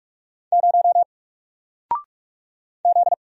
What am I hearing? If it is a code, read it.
Morse code, heard as a keyed tone, about 46 wpm, 700 Hz. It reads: 0 Z
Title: Single Letter-Number 45wpm